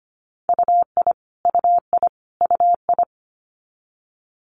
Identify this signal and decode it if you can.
Morse code, heard as a keyed tone, about 25 wpm, 700 Hz. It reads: UI UI UI
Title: Multi-letter Phonemes Repeated 3x 25wpm